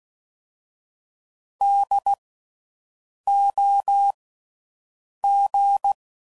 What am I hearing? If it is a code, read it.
Morse code, heard as a keyed tone, about 16 wpm, 800 Hz. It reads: D O G
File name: Suffixes / General / Houses